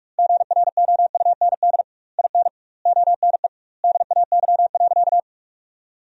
Morse code, heard as a keyed tone, about 45 wpm, 700 Hz. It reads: GROUND IN ONE DAY.
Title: 2x Word-Spacing - Sentences from Top 500 Words - 45wpm